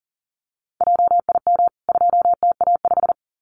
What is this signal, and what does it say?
Morse code, heard as a keyed tone, about 40 wpm, 700 Hz. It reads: JIM 2TA5
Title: CWT Exchange 40wpm